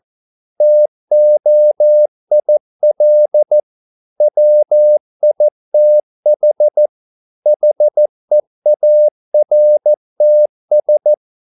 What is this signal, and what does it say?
Morse code, heard as a keyed tone, about 14 wpm, 600 Hz. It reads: TOIL WITH HEARTS